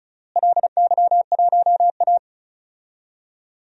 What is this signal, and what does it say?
Morse code, heard as a keyed tone, about 36 wpm, 700 Hz. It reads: LY1A
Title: DXCC-100 Call Signs - 4 Characters - Repeated 3x - 35wpm